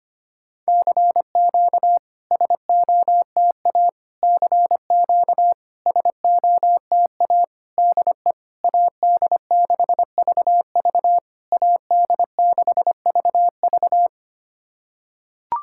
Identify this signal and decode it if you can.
Morse code, heard as a keyed tone, about 25 wpm, 690 Hz. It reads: CQ SOTA CQ SOTA DE AD6VV AD6VV
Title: SOTA QSO Practice 25wpm